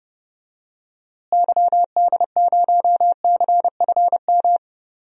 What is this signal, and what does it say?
Morse code, heard as a keyed tone, about 30 wpm, 700 Hz. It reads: YD0CFM
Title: DXCC-100 Call Signs - 6 Characters - Repeated 3x - 30wpm